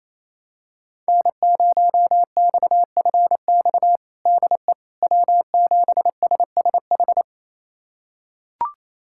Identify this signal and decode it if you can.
Morse code, heard as a keyed tone, about 27 wpm, 700 Hz. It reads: N0XFX DE W7SSH
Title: USA Callsign DE Callsign - 5 Characters - 28wpm